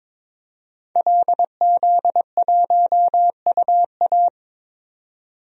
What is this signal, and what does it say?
Morse code, heard as a keyed tone, about 22 wpm, 700 Hz. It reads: LZ1UA
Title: DXCC-100 Call Signs - 5 Characters - 22wpm